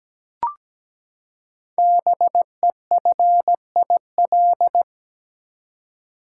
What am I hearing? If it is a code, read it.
Morse code, heard as a keyed tone, about 17 wpm, 700 Hz. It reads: BEFIL
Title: Code Groups - 5 Letters - 17wpm